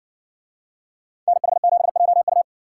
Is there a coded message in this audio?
Yes: Morse code from 1.3 to 2.4 s, heard as a keyed tone, about 57 wpm, 700 Hz. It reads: NH7JU